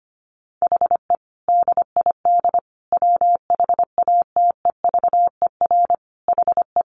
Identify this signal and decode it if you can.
Morse code, heard as a keyed tone, about 25 wpm, 700 Hz. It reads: HE DID WHATEVER HE
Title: Sentences from Top 1100 Words 25wpm